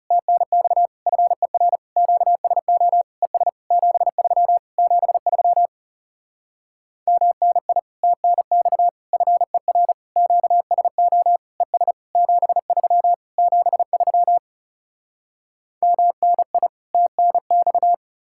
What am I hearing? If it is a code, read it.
Morse code, heard as a keyed tone, about 37 wpm, 700 Hz. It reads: TNX FER QSO ES 73 73 MNI TNX FER QSO ES 73 73 MNI TNX
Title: Speed-Racing Common QSO Phrases Encore - 30wpm